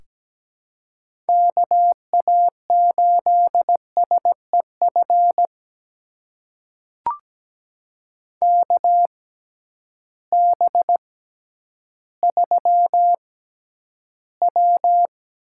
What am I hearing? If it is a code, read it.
Morse code, heard as a keyed tone, about 17 wpm, spaced as for 10 wpm, 700 Hz. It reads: KA8SEF KB3W
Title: USA Call Signs - 6 Characters - 6x Character Spacing - 17wpm